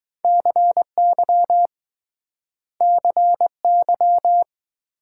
Morse code, heard as a keyed tone, about 20 wpm, 700 Hz. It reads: CY CY